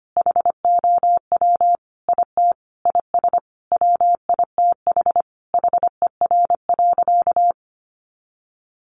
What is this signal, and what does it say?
Morse code, heard as a keyed tone, about 25 wpm, 700 Hz. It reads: HOW IT IS WITH HER.
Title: Sentences from Top 100 Words 25wpm